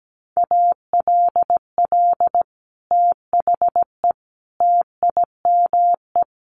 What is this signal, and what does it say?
Morse code, heard as a keyed tone, about 17 wpm, 700 Hz. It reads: ALL THE TIME
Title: Sets of 3 Words 17wpm